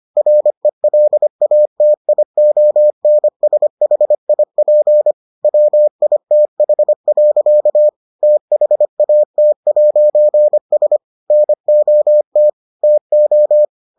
Morse code, heard as a keyed tone, about 25 wpm, 600 Hz. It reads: RELATIONSHIP WITH. THAT'S NOT TO